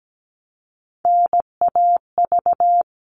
Morse code, heard as a keyed tone, about 17 wpm, 700 Hz. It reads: NAV
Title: All Cut Numbers - 3-Digit 17wpm